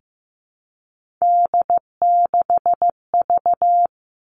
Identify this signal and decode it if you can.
Morse code, heard as a keyed tone, about 15 wpm, 700 Hz. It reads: D6V
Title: Three Challenging Letter-Number Combinations 15wpm